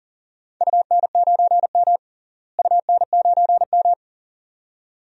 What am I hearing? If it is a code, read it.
Morse code, heard as a keyed tone, about 40 wpm, 700 Hz. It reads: UN9M UN9M
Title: DXCC-100 Call Signs - 4 Characters - Repeated 2x - 40wpm